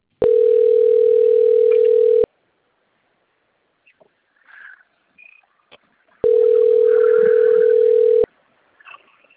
Ringback tone (North America), 440 + 480 Hz.